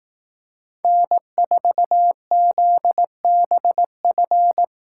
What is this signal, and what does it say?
Morse code, heard as a keyed tone, about 18 wpm, 700 Hz. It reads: N4ZBF
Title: Speed Racing - USA 1x3 Call Signs 15wpm